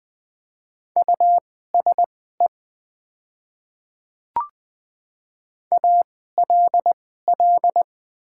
Morse code, heard as a keyed tone, about 20 wpm, spaced as for 15 wpm, 700 Hz. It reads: USE ALL
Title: 2x Character Spacing - Top 200 Words - 20wpm